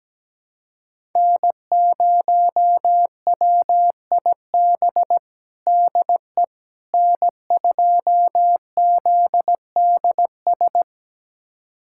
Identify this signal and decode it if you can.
Morse code, heard as a keyed tone, about 17 wpm, 700 Hz. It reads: N0WIB DE N2ZDS
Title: USA Callsign DE Callsign - 5 Characters - 17wpm